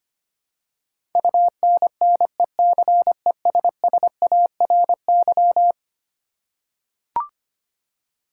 Morse code, heard as a keyed tone, about 25 wpm, 700 Hz. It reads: UNNECESSARY
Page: morse-code-ninja-practice-25wpm